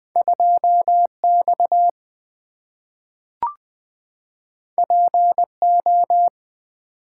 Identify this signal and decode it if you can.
Morse code, heard as a keyed tone, about 20 wpm, 700 Hz. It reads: 2 X P O
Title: Two Letter-Number Combinations - 20wpm